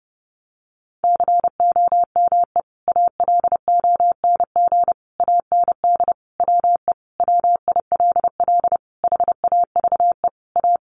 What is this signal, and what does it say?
Morse code, heard as a keyed tone, about 30 wpm, 700 Hz. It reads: COME ALONG AND WE WILL HAVE A
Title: Sentences from Top 1500 Words 30wpm